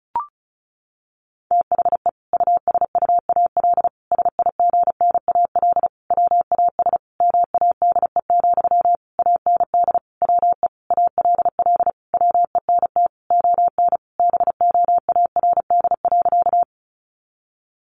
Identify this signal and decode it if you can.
Morse code, heard as a keyed tone, about 34 wpm, 700 Hz. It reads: THE USUAL SIGNAL WAS MADE, AND WE ALL WENT ON BOARD.
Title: Sentences from Top 1400 Words 35wpm